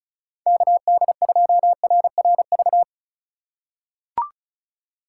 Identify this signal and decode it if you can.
Morse code, heard as a keyed tone, about 36 wpm, 700 Hz. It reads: KD2RRV
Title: USA Call Signs - 6 Characters - 4x Character Spacing - 35wpm